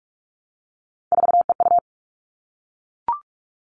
Morse code, heard as a keyed tone, about 44 wpm, 700 Hz. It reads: 4EU
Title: All Cut Numbers - 3-Digit 45wpm